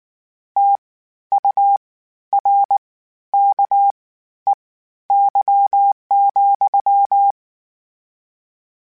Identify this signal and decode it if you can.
Morse code, heard as a keyed tone, about 19 wpm, spaced as for 12 wpm, 800 Hz. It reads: TURKEY,
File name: Misc / Misc-mp3s / WordPairs sp @18